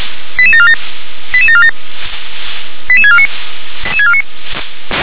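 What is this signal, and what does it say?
Five-tone selective call (ZVEI), heard as quick runs of five tones, tones of about 70 ms. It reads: 88747 88747 88649
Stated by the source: ZVEI five-tone call received via SDR.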